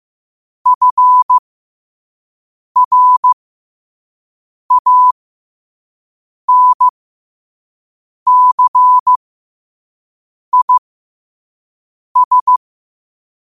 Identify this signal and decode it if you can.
Morse code, heard as a keyed tone, about 15 wpm, spaced as for 5 wpm, 1000 Hz. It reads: FRANCIS
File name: qotdfiles / 5wpm